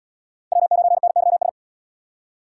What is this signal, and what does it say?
Morse code, heard as a keyed tone, about 70 wpm, 700 Hz. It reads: W9NYS